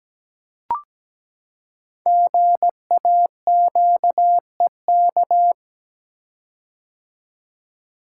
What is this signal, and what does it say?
Morse code, heard as a keyed tone, about 17 wpm, 700 Hz. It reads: GAQEK
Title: Code Groups - 5 Letters - 17wpm